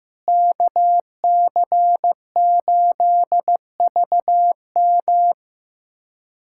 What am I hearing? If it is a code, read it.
Morse code, heard as a keyed tone, about 16 wpm, 700 Hz. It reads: KC8VM
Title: USA Callsign DE Callsign - 5 Characters - 15wpm